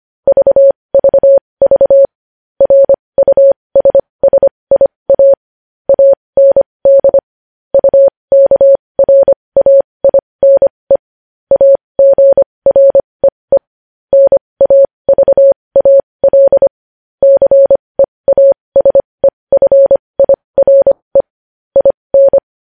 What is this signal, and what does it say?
Morse code, heard as a keyed tone, about 25 wpm, spaced as for 20 wpm, 550 Hz. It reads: VVV RUSSIA AND UKRAINE AGREE NAVAL CEASEFIRE IN
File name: podcast / News headlines-1x-25x20